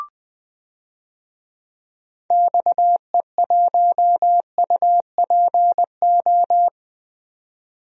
Morse code, heard as a keyed tone, about 21 wpm, 700 Hz. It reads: XE1UPO
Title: DXCC-100 Call Signs - 6 Characters - Repeated 3x - 20wpm